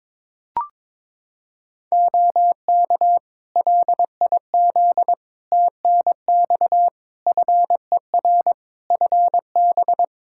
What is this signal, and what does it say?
Morse code, heard as a keyed tone, about 22 wpm, 700 Hz. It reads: OK LIZ TNX FER FB
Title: Practice QSO - Exchange 4 - Ending - 22wpm